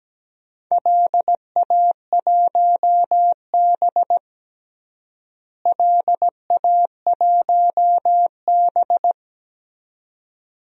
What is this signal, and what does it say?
Morse code, heard as a keyed tone, about 17 wpm, 700 Hz. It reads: LA1B LA1B
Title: DXCC-100 Call Signs - 4 Characters - Repeated 2x - 17wpm